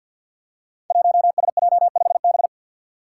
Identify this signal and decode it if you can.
Morse code, heard as a keyed tone, about 52 wpm, 700 Hz. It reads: 1SJ5B
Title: Five Challenging Letter-Number Combinations 50wpm